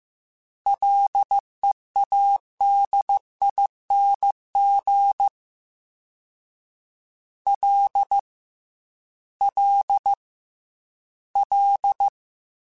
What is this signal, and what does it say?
Morse code, heard as a keyed tone, about 15 wpm, 800 Hz. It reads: LEADING L L L
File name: Combinations / Combinations-mp3s / Combinations@13